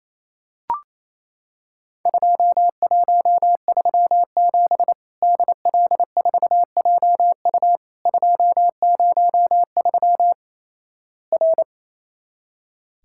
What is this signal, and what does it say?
Morse code, heard as a keyed tone, about 28 wpm, 700 Hz. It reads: 2137 DL4JU 203 R
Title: Worked All Europe DX Contest - QTC Exchange - 28wpm